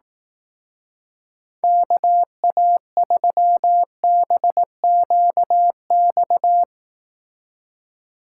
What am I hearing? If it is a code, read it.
Morse code, heard as a keyed tone, about 18 wpm, 700 Hz. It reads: KA3BQX
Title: Speed Racing - USA 2x3 Call Signs 15wpm